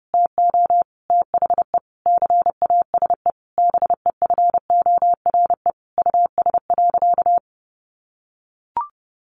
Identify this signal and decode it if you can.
Morse code, heard as a keyed tone, about 30 wpm, 700 Hz. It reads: TO THE CASE BEFORE US.
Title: Sentences from Top 1400 Words 30wpm